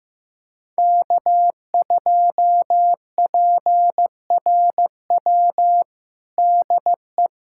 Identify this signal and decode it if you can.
Morse code, heard as a keyed tone, about 15 wpm, 700 Hz. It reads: K2PRW DE
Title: USA Callsign DE Callsign - 5 Characters - 15wpm